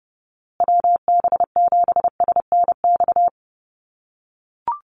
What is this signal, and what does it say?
Morse code, heard as a keyed tone, about 30 wpm, 700 Hz. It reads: WB7SNX
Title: USA Call Signs Repeated 3x 30wpm